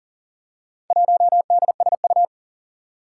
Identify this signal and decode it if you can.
Morse code, heard as a keyed tone, about 42 wpm, 700 Hz. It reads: 1DSU
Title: Four Challenging Letter-Number Combinations 40wpm